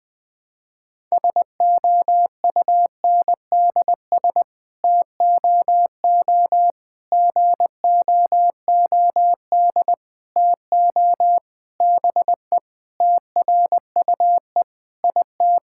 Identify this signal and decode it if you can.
Morse code, heard as a keyed tone, about 20 wpm, 700 Hz. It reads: SOUNDS TOO GOOD TO BE TRUE IT
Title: Sets of 9 Words - 20wpm